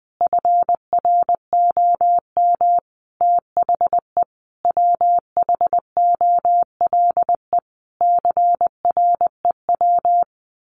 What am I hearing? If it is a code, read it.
Morse code, heard as a keyed tone, about 20 wpm, 700 Hz. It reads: FROM THE WHOLE CREW
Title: Sentences from Top 1500 Words 20wpm